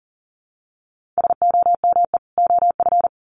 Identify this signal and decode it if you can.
Morse code, heard as a keyed tone, about 37 wpm, 700 Hz. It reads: SOME OF
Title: Sets of 2 Words 40wpm